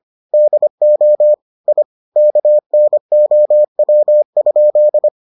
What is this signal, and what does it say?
Morse code, heard as a keyed tone, about 25 wpm, 600 Hz. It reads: DO I KNOW?